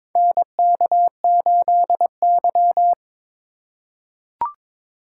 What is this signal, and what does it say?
Morse code, heard as a keyed tone, about 22 wpm, 700 Hz. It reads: NK8Y
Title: USA Call Signs - 4 Characters - 3x Character Spacing - 22wpm